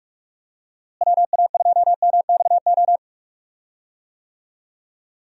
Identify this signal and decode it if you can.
Morse code, heard as a keyed tone, about 41 wpm, 700 Hz. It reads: WA2MXO
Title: USA Call Signs - 6 Characters - Repeated 2x - 45wpm